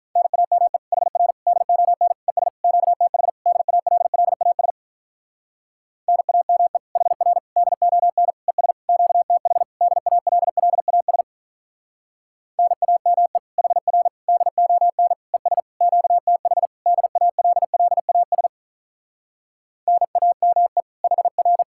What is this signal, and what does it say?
Morse code, heard as a keyed tone, about 44 wpm, 700 Hz. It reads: NAME HR DON ES QTH DALLAS NAME HR DON ES QTH DALLAS NAME HR DON ES QTH DALLAS NAME HR
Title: Speed-Racing Common QSO Phrases Encore - 35wpm